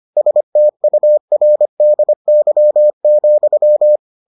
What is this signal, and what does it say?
Morse code, heard as a keyed tone, about 25 wpm, 600 Hz. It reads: STURDY,